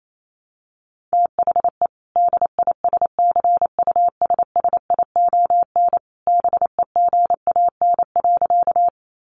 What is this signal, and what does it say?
Morse code, heard as a keyed tone, about 28 wpm, 700 Hz. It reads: THE DISCUSSION BEGAN.
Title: Sentences from Top 1100 Words 28wpm